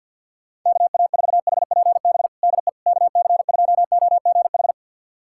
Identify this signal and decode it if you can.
Morse code, heard as a keyed tone, about 47 wpm, 700 Hz. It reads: KA4HPB DE KC2OGH
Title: USA Callsign DE Callsign 50wpm